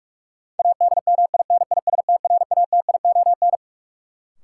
Morse code, heard as a keyed tone, about 43 wpm, 700 Hz. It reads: ADMINISTRATION
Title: Sets of 5 Words - New Word Drill - 45wpm